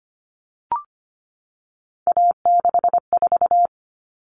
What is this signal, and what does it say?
Morse code, heard as a keyed tone, about 25 wpm, 700 Hz. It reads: A64
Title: Common Cut Numbers - 3-Digit 25wpm